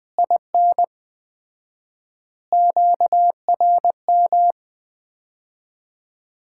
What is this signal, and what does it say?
Morse code, heard as a keyed tone, about 21 wpm, 700 Hz. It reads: IN QRM
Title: Common QSO Phrases - 4x Word-Spacing - 20wpm